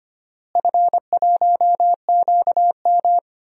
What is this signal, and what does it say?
Morse code, heard as a keyed tone, about 25 wpm, 700 Hz. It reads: F1QM
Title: DXCC-100 Call Signs - 4 Characters - Repeated 2x - 25wpm